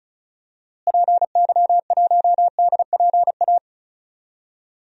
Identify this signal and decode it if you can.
Morse code, heard as a keyed tone, about 34 wpm, 700 Hz. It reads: PY1DPA
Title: DXCC-100 Call Signs - 6 Characters - Repeated 2x - 35wpm